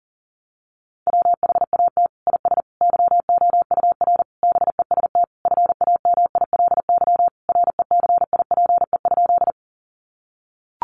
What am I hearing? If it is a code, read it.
Morse code, heard as a keyed tone, about 40 wpm, 700 Hz. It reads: WHAT IS YOUR BEST FAMILY RECIPE?
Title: Sentences from Top 1200 Words 40wpm